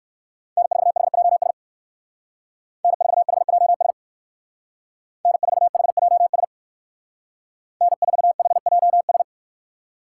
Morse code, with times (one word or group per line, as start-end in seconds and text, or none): 0.6-1.5 N4HJS
2.8-3.9 N4HJS
5.3-6.4 N4HJS
7.8-9.2 N4HJS